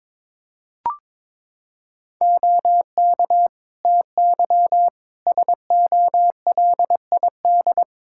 Morse code, heard as a keyed tone, about 22 wpm, 700 Hz. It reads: OK TY SOLID
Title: K1USN SST QSO Practice 22wpm